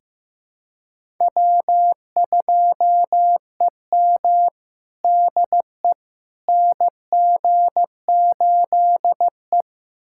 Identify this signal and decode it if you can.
Morse code, heard as a keyed tone, about 16 wpm, 700 Hz. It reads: W2EM DE NG8E
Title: USA Callsign DE Callsign - 4 Characters - 15wpm